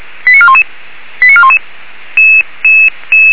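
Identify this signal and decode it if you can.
Five-tone selective call (ZVEI), heard as quick runs of five tones, tones of about 70 ms. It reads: 89411 89411
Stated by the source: ZVEI five-tone selective call 89413 received on an SDR.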